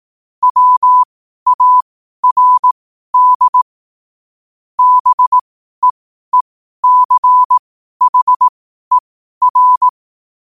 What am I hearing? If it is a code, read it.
Morse code, heard as a keyed tone, about 18 wpm, spaced as for 12 wpm, 1000 Hz. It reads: WARD BEECHER
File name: qotdfiles / 12wpm